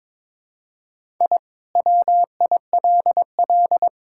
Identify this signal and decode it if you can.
Morse code, heard as a keyed tone, about 22 wpm, 700 Hz. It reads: I WILL
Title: Sentences from Top 1000 Words - Review Part 5 - 22wpm